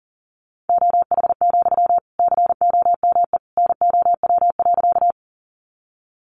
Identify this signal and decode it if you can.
Morse code, heard as a keyed tone, about 40 wpm, 700 Hz. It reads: OH, COME NOW.
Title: Sentences from Top 500 Words 40wpm